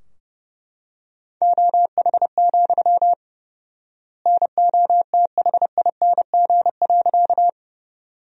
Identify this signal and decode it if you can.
Morse code, heard as a keyed tone, about 30 wpm, 700 Hz. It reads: OH, NOTHING.
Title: 4x Word-Spacing - Sentences from Top 500 Words - 30wpm